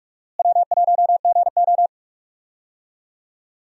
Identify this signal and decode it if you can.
Morse code, heard as a keyed tone, about 42 wpm, 700 Hz. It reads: W1GO
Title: USA Call Signs - 4 Characters - Repeated 3x - 45wpm